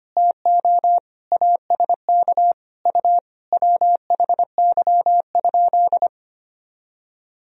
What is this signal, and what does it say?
Morse code, heard as a keyed tone, about 25 wpm, 700 Hz. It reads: TO ASK U WHY?
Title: Sentences from Top 200 Words and Abbreviations 25wpm